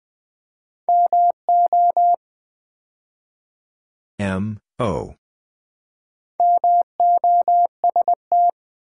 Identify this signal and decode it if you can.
Morse code, heard as a keyed tone, about 20 wpm, 700 Hz. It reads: MO MOST
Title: Build A Word - 2 Letters At A Time - 6-Letter Words - 20wpm